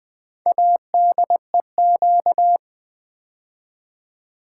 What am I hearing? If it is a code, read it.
Morse code, heard as a keyed tone, about 20 wpm, 700 Hz. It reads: ADEQ
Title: Build A Word - 2 Letters At A Time - 8-Letter Words - 20wpm